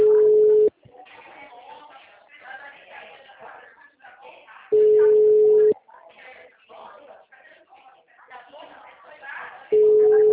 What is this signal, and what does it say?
Ringback tone (Europe, 425 Hz), with speech.
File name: out- -2015- - - .6538